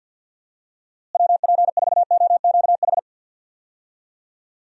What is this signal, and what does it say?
Morse code, heard as a keyed tone, about 46 wpm, 700 Hz. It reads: WP4OQH
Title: USA Call Signs - 6 Characters - Repeated 2x - 50wpm